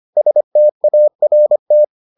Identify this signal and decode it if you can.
Morse code, heard as a keyed tone, about 25 wpm, 600 Hz. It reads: START